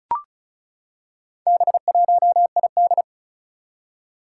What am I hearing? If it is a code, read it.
Morse code, heard as a keyed tone, about 36 wpm, 700 Hz. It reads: B1ID
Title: Four Challenging Letter-Number Combinations 35wpm